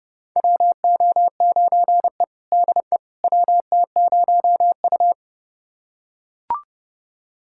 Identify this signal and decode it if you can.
Morse code, heard as a keyed tone, about 30 wpm, 700 Hz. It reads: WO9E DE WT0U
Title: USA Callsign DE Callsign - 4 Characters - 30wpm